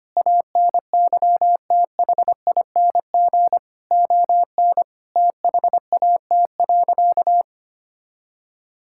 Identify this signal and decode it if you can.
Morse code, heard as a keyed tone, about 25 wpm, 700 Hz. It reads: ANYTHING ON THAT.
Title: Sentences from Top 1000 Words - Review Part 5 - 25wpm